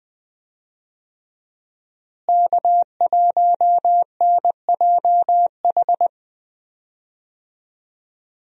Morse code, heard as a keyed tone, about 20 wpm, 700 Hz. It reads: K1NJH
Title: USA Call Signs - 5 Characters - Repeated 3x - 20wpm